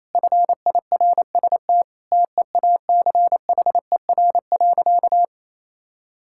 Morse code, heard as a keyed tone, about 28 wpm, 700 Hz. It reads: FIRST TEACHER.
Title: Sentences from Top 1000 Words - Review Part 6 - 28wpm